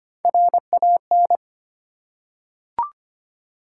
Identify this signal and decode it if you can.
Morse code, heard as a keyed tone, about 25 wpm, 700 Hz. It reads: RAN
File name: Three Letter-Number Combinations - 25wpm